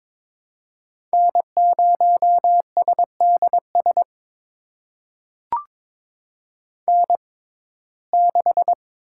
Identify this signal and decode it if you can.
Morse code, heard as a keyed tone, about 22 wpm, spaced as for 16 wpm, 700 Hz. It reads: N0SDS N6
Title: USA Call Signs - 6x Character Spacing - 22wpm